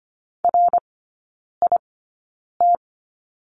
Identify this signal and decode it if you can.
Morse code, heard as a keyed tone, about 25 wpm, spaced as for 9 wpm, 690 Hz. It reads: RIT
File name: Lesson 14e - Number 2 - Three Characters 25wpm